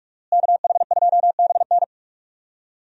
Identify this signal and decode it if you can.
Morse code, heard as a keyed tone, about 42 wpm, 700 Hz. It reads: KH2BN
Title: USA Call Signs - 5 Characters - Repeated 2x - 45wpm